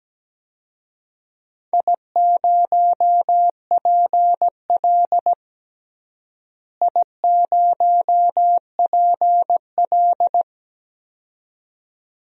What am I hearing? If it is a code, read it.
Morse code, heard as a keyed tone, about 17 wpm, 700 Hz. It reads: I0PL I0PL
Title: DXCC-100 Call Signs - 4 Characters - Repeated 2x - 17wpm